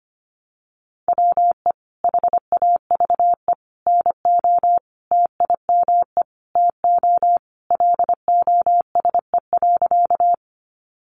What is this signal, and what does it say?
Morse code, heard as a keyed tone, about 25 wpm, 700 Hz. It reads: WE HAVE NO TIME TO LOSE.
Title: Sentences from Top 1100 Words 25wpm